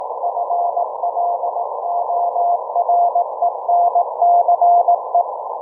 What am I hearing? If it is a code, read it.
Morse code, heard as a keyed tone, about 18 wpm, 690 Hz, with noise. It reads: FLORENCE